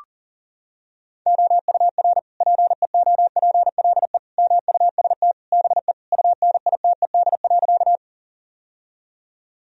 Morse code, heard as a keyed tone, about 40 wpm, 700 Hz. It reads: OUR PEOPLE MUST BE UNITED.
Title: Sentences from Top 1000 Words - Review Part 4 - 40wpm